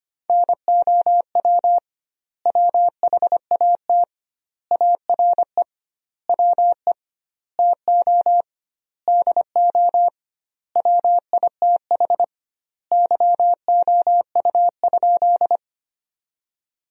Morse code, heard as a keyed tone, about 25 wpm, 700 Hz. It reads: NOW WHAT ARE WE TO DO WITH YOU?
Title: 2x Word-Spacing - Sentences from Top 100 Words - 25wpm